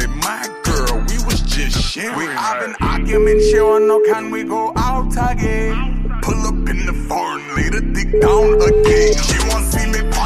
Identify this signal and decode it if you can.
Ringback tone (Europe, 425 Hz), with noise.